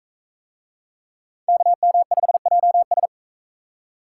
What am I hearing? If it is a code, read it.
Morse code, heard as a keyed tone, about 39 wpm, 700 Hz. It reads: KM5JS